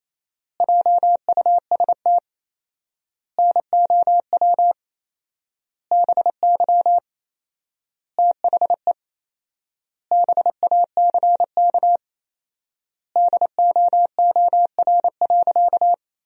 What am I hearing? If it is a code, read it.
Morse code, heard as a keyed tone, about 28 wpm, 700 Hz. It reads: JUST NOW BY THE BACK DOOR.
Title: 4x Word-Spacing - Sentences from Top 300 Words - 28wpm